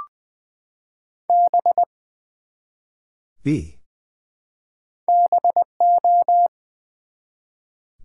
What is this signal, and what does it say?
Morse code, heard as a keyed tone, about 20 wpm, 700 Hz. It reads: B BO
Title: Build A Word - 1 Letter At A Time - 3-Letter Words - 20wpm